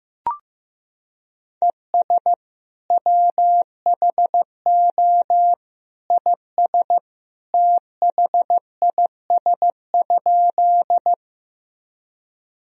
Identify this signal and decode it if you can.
Morse code, heard as a keyed tone, about 15 wpm, 700 Hz. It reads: ES WHO IS THIS?
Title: Sentences from Top 100 Words and Abbreviations 15wpm